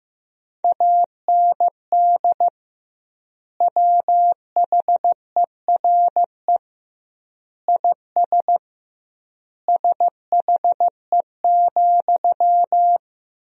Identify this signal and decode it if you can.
Morse code, heard as a keyed tone, about 15 wpm, 700 Hz. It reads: AND WHERE IS SHE,
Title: 2x Word-Spacing - Sentences from Top 200 Words - 15wpm